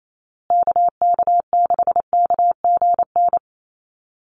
Morse code, heard as a keyed tone, about 28 wpm, 700 Hz. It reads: KK6KGN
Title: USA Call Signs Repeated 3x 28wpm